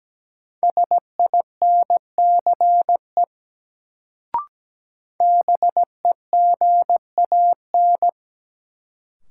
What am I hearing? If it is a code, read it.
Morse code, heard as a keyed tone, about 17 wpm, 700 Hz. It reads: SINCE BEGAN
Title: Sentences from Top 300 Words - New Word Drill - 17wpm